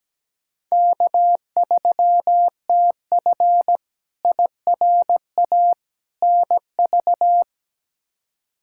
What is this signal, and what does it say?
Morse code, heard as a keyed tone, about 17 wpm, 700 Hz. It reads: K3TF IRA NV
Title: K1USN SST QSO Practice - 17wpm